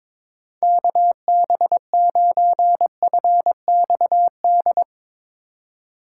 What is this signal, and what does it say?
Morse code, heard as a keyed tone, about 22 wpm, 700 Hz. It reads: KB9FXD
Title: USA Call Signs - 6 Characters - 3x Character Spacing - 22wpm